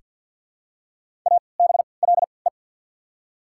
Morse code, heard as a keyed tone, about 54 wpm, spaced as for 29 wpm, 690 Hz. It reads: ABLE
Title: 3x Character Spacing - Top 500 Words - 50wpm